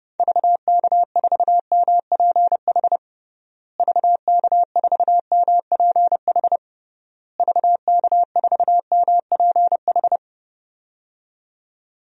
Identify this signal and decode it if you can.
Morse code, heard as a keyed tone, about 32 wpm, 700 Hz. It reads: VK4MPH VK4MPH VK4MPH
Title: DXCC-100 Call Signs - 6 Characters - Repeated 3x - 30wpm